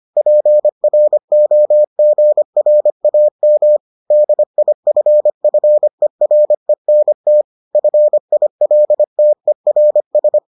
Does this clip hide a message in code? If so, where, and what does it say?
Yes: Morse code from 0.2 to 10.4 s, heard as a keyed tone, about 25 wpm, 600 Hz. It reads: PROGRAM DIFFERENT FILTERS